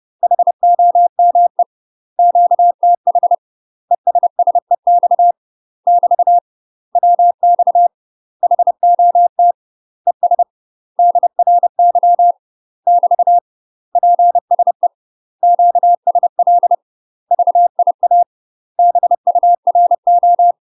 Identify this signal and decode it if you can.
Morse code, heard as a keyed tone, about 30 wpm, 700 Hz. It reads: HOME QTH ESSEX <BT> WX HOT ES DRY <BT> PSE QSL VIA BURO
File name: UK MASTER QSOs TRAINING / 30WPM / UK MASTER QSOs 0036